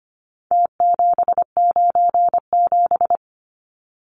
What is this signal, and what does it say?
Morse code, heard as a keyed tone, about 25 wpm, 700 Hz. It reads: T797
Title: Numbers - Contest Style - 4-Digit 25wpm